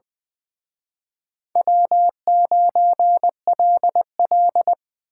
Morse code, heard as a keyed tone, about 24 wpm, 700 Hz. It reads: W9LL
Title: Speed Racing - USA 1x2 Call Signs 20wpm